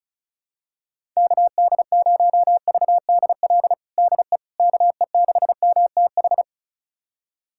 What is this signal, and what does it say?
Morse code, heard as a keyed tone, about 36 wpm, 700 Hz. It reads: KD0VDL DE KE6MTH
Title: USA Callsign DE Callsign - 6 Characters - 35wpm